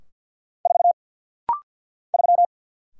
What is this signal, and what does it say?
Morse code, heard as a keyed tone, about 49 wpm, 700 Hz. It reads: 4 3
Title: Single Letter-Number - Rapid-Fire 50wpm